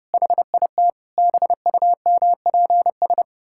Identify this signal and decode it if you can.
Morse code, heard as a keyed tone, about 30 wpm, 700 Hz. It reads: HIT BUMPS
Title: Pangram Challenge v2 - 30wpm